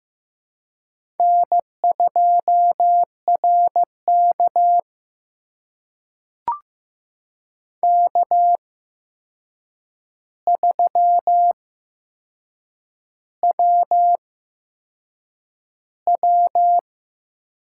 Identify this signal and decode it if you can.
Morse code, heard as a keyed tone, about 15 wpm, spaced as for 6 wpm, 700 Hz. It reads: N2RK K3WW
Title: USA Call Signs - 4 Characters - 8x Character Spacing - 15wpm